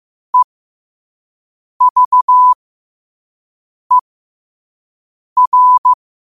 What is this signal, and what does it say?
Morse code, heard as a keyed tone, about 15 wpm, spaced as for 5 wpm, 1000 Hz. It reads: EVER
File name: qotdfiles / 5wpm